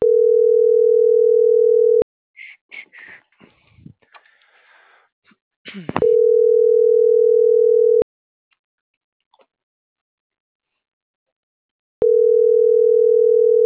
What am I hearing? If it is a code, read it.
Ringback tone (North America), 440 + 480 Hz.